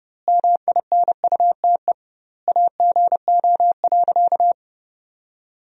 Morse code, heard as a keyed tone, about 32 wpm, 700 Hz. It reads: MINUTE AGO.
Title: 2x Word-Spacing - Sentences from Top 500 Words - 30wpm